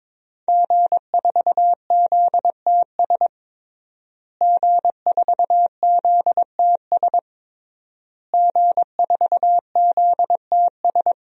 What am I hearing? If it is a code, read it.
Morse code, heard as a keyed tone, about 22 wpm, 700 Hz. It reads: G4ZTS G4ZTS G4ZTS
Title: DXCC-100 Call Signs - 5 Characters - Repeated 3x - 22wpm